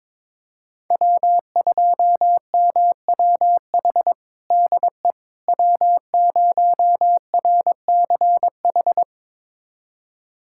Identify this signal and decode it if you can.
Morse code, heard as a keyed tone, about 22 wpm, 700 Hz. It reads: W2MWH DE W0RCH
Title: USA Callsign DE Callsign - 5 Characters - 22wpm